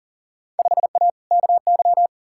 Morse code, heard as a keyed tone, about 40 wpm, 700 Hz. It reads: 5A KY
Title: Field Day Exchanges - 40wpm